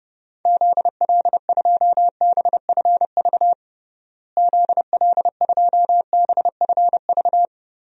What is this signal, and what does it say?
Morse code, heard as a keyed tone, about 30 wpm, 700 Hz. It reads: ZL2BFV ZL2BFV
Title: DXCC-100 Call Signs - 6 Characters - Repeated 3x - 30wpm